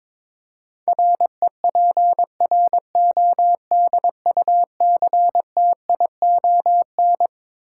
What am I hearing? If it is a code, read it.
Morse code, heard as a keyed tone, about 22 wpm, 700 Hz. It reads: REPRODUCTION
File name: Twelve Letter Words - 22wpm